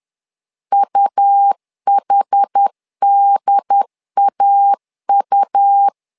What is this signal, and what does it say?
Morse code, heard as a keyed tone, about 10 wpm, 800 Hz. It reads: UHDAU
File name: LEZIONI / 2 UHDN / 2 UHD 10wpm 1826